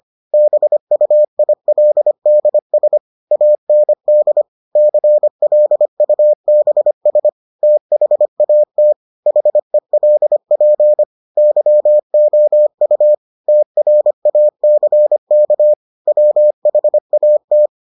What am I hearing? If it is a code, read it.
Morse code, heard as a keyed tone, about 25 wpm, 600 Hz. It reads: BUILDS AND CLUBS THAT HELP YOU TRACK WHAT